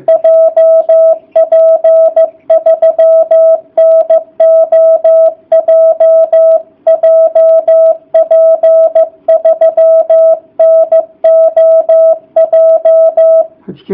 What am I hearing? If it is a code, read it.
Morse code, heard as a keyed tone, about 15 wpm, 660 Hz. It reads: JP3NOJ JP3NOJ